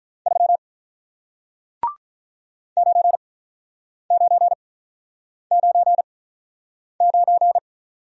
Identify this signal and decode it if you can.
Morse code, heard as a keyed tone, about 46 wpm, spaced as for 9 wpm, 700 Hz. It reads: 3 9999